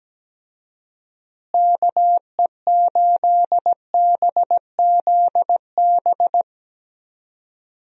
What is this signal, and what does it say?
Morse code, heard as a keyed tone, about 17 wpm, 700 Hz. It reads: KE8BZB
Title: USA Call Signs - 6 Characters - Repeated 3x - 17wpm